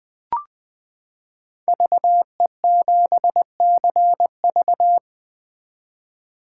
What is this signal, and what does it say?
Morse code, heard as a keyed tone, about 20 wpm, 700 Hz. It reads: VE7CV
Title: 2023 SOTA Chaser Call Signs - 20wpm